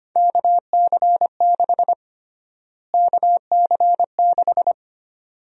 Morse code, heard as a keyed tone, about 25 wpm, 700 Hz. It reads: KC6 KC6
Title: USA Call Sign Prefixes - Repeated 2x - 25wpm